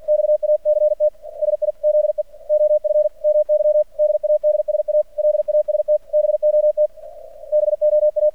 Morse code, heard as a keyed tone, about 48 wpm, 600 Hz. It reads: XACT SIZE OF MY DADSA LAST BOT BOA
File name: archived / CRSnet